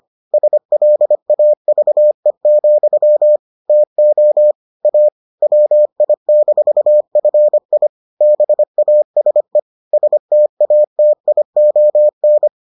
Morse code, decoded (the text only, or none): SLAVE, TO A WI-FI BASE STATION